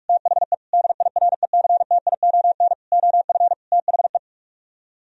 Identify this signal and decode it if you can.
Morse code, heard as a keyed tone, about 44 wpm, 700 Hz. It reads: THE DIRECTION OF THE